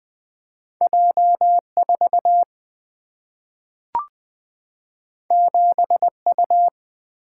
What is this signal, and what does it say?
Morse code, heard as a keyed tone, about 20 wpm, 700 Hz. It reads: J4 7U
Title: Single-Letter Single-Number 20wpm